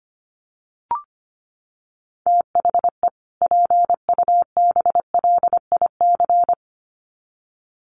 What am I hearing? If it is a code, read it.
Morse code, heard as a keyed tone, about 25 wpm, 700 Hz. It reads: THE PUBLIC
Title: Sets of 2 Words 25wpm